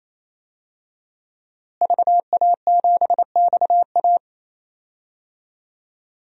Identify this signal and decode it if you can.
Morse code, heard as a keyed tone, about 28 wpm, 700 Hz. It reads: VA7XA
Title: DXCC-100 Call Signs - 5 Characters - Repeated 3x - 28wpm